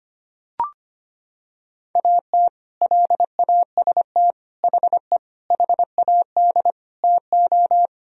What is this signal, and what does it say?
Morse code, heard as a keyed tone, about 25 wpm, 700 Hz. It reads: AT LAST HE HAD TO
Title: Sentences from Top 1000 Words - Review Part 4 - 25wpm